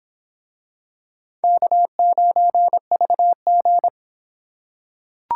Morse code, heard as a keyed tone, about 26 wpm, 700 Hz. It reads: K9VG